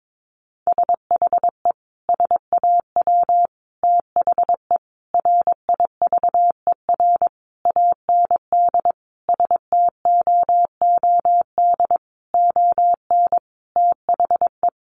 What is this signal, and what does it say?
Morse code, heard as a keyed tone, about 22 wpm, 700 Hz. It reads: SHE SAW THE RIVER AND STOOD ON THE
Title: Sentences from Top 1200 Words 22wpm